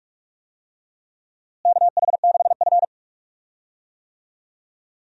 Morse code, heard as a keyed tone, about 42 wpm, 700 Hz. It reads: KH6F
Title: USA Call Signs - 4 Characters - Repeated 2x - 45wpm